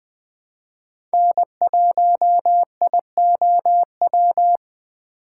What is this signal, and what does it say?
Morse code, heard as a keyed tone, about 20 wpm, 700 Hz. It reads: N1IOW
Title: USA Call Signs - 5 Characters - 2x Character Spacing - 20wpm